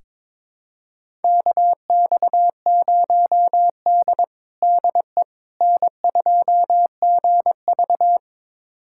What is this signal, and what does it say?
Morse code, heard as a keyed tone, about 22 wpm, 700 Hz. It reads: KX0D DE N2GV
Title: USA Callsign DE Callsign - 4 Characters - 22wpm